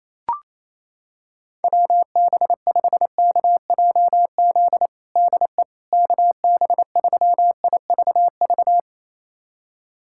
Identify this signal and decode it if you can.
Morse code, heard as a keyed tone, about 28 wpm, 700 Hz. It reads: WB5KJZ DE KB3IVV
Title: USA Callsign DE Callsign - 6 Characters - 28wpm